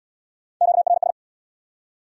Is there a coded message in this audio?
Yes: Morse code from 0.6 to 1.1 s, heard as a keyed tone, about 72 wpm, 700 Hz. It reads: QRS